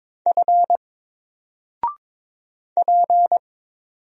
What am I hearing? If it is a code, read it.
Morse code, heard as a keyed tone, about 22 wpm, 700 Hz. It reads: F P
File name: Single-Letter-Number-22wpm